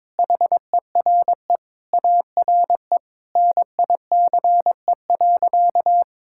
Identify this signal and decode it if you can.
Morse code, heard as a keyed tone, about 22 wpm, 700 Hz. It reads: HERE ARE NICE.
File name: practice / 22wpm / Sentences with 3-Word Phrases - 22wpm